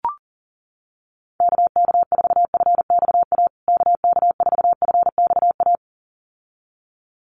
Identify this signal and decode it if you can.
Morse code, heard as a keyed tone, about 40 wpm, 700 Hz. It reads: KK4FXA KK4FXA
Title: USA Call Signs Repeated 2x 40wpm